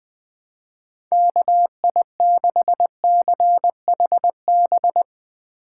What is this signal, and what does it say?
Morse code, heard as a keyed tone, about 20 wpm, 700 Hz. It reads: KI6CHB